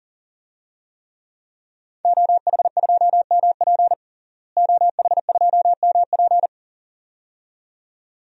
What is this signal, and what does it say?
Morse code, heard as a keyed tone, about 40 wpm, 700 Hz. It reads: OH2MP OH2MP
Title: DXCC-100 Call Signs - 5 Characters - Repeated 2x - 40wpm